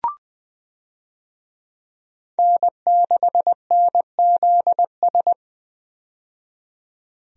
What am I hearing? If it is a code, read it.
Morse code, heard as a keyed tone, about 21 wpm, 700 Hz. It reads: N6NZS
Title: USA Call Signs - 5 Characters - 20wpm